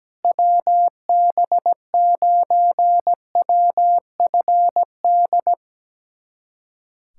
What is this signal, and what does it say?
Morse code, heard as a keyed tone, about 17 wpm, 700 Hz. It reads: WB9WFD
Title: USA Callsign DE Callsign 17wpm